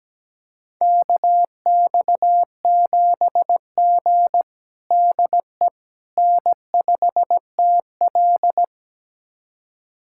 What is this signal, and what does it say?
Morse code, heard as a keyed tone, about 17 wpm, 700 Hz. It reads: KX7G DE N5TL
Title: USA Callsign DE Callsign - 4 Characters - 17wpm